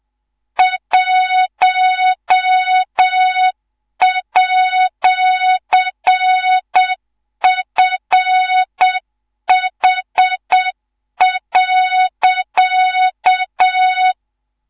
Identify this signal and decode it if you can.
Morse code, heard as a keyed tone, about 7 wpm, 3700 Hz, with noise. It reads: 1@FH.
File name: Morze-HA2MN / hang / 19 GYAKORLAT